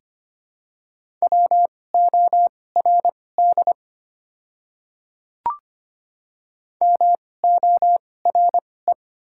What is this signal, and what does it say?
Morse code, heard as a keyed tone, about 25 wpm, spaced as for 18 wpm, 700 Hz. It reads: WORD MORE
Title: 2x Character Spacing - Top 100 Words - 25wpm